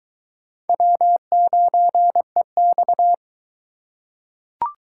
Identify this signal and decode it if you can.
Morse code, heard as a keyed tone, about 22 wpm, 700 Hz. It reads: W9EX